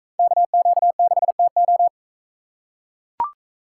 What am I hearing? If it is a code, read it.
Morse code, heard as a keyed tone, about 44 wpm, 700 Hz. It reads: KQ6TO